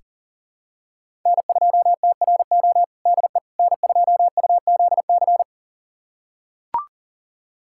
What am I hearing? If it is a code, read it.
Morse code, heard as a keyed tone, about 40 wpm, 700 Hz. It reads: N2TRO DE N2UZC
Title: USA Callsign DE Callsign - 5 Characters - 40wpm